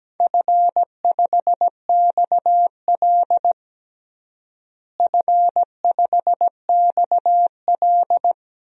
Morse code, heard as a keyed tone, about 17 wpm, 700 Hz. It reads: F5XL F5XL
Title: DXCC-100 Call Signs - 4 Characters - Repeated 3x - 17wpm